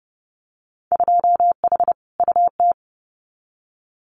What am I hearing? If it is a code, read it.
Morse code, heard as a keyed tone, about 32 wpm, 700 Hz. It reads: 2H UT
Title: Winter Field Day Exchanges Encore 30wpm